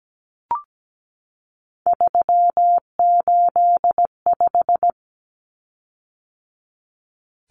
Morse code, heard as a keyed tone, about 17 wpm, 700 Hz. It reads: 385
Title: Common Cut Numbers - 3-Digit 17wpm